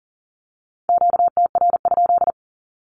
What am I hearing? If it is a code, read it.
Morse code, heard as a keyed tone, about 40 wpm, 700 Hz. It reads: QTR?
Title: Q-Codes with Spoken Meaning - 40wpm